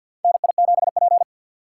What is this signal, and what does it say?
Morse code, heard as a keyed tone, about 54 wpm, 700 Hz. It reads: NI7P
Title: USA Call Signs - 4 Characters - 3x Character Spacing - 50wpm